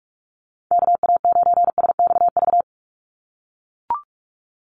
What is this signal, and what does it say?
Morse code, heard as a keyed tone, about 44 wpm, 700 Hz. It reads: KA9SXV
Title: Call Signs - 45wpm